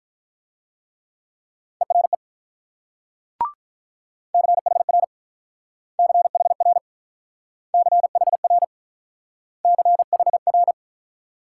Morse code, heard as a keyed tone, about 48 wpm, 690 Hz. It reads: ERE CHR CHR CHR CHR